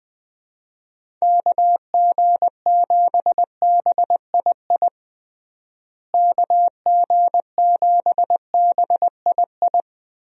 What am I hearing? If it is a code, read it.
Morse code, heard as a keyed tone, about 20 wpm, 700 Hz. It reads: KG7BII KG7BII
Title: USA Call Signs - 6 Characters - Repeated 2x - 20wpm